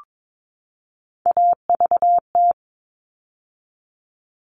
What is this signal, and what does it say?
Morse code, heard as a keyed tone, about 22 wpm, 700 Hz. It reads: AVT
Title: All Cut Numbers - 3-Digit 22wpm